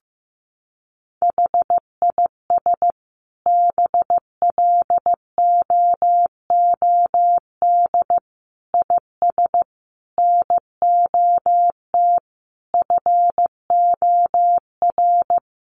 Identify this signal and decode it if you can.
Morse code, heard as a keyed tone, about 15 wpm, 700 Hz. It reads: HIS BLOOD IS NOT FOR
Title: Sentences from Top 1400 Words 15wpm